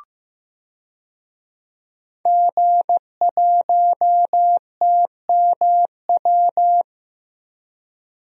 Morse code, heard as a keyed tone, about 15 wpm, 700 Hz. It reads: G1TMW
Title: DXCC-100 Call Signs - 5 Characters - 15wpm